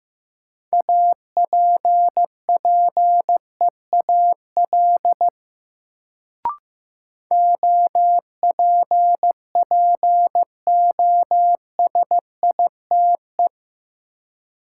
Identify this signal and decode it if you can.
Morse code, heard as a keyed tone, about 15 wpm, 700 Hz. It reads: APPEAL OPPOSITE
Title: Sentences from Top 1300 Words - New Word Drill - 15wpm